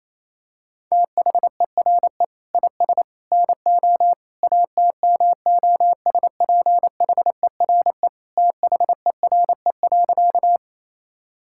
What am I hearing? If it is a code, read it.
Morse code, heard as a keyed tone, about 28 wpm, 700 Hz. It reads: THERE IS NO ATMOSPHERE THERE.
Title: Sentences from Top 1600 Words 28wpm